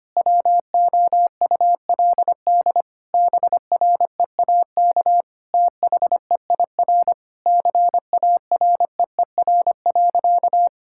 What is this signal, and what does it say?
Morse code, heard as a keyed tone, about 25 wpm, 700 Hz. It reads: WOULD BREAK THEIR CAREER.
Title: Sentences from Top 1000 Words - Review Part 6 - 25wpm